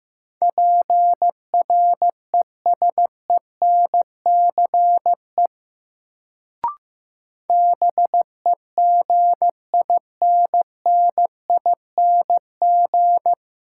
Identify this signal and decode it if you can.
Morse code, heard as a keyed tone, about 15 wpm, 700 Hz. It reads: PRESENCE BEGINNING
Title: Sets of 3 Words - New Word Drill - 15wpm